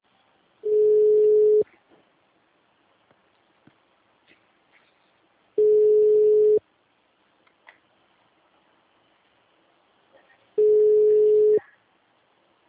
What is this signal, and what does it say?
Ringback tone (Europe, 425 Hz), with speech.
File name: out- -2038- - - .1390